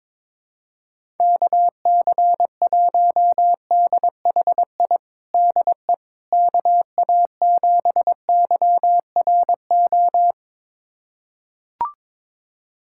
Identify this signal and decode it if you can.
Morse code, heard as a keyed tone, about 22 wpm, 700 Hz. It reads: KC1DHI DE KA7YRO
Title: USA Callsign DE Callsign - 6 Characters - 22wpm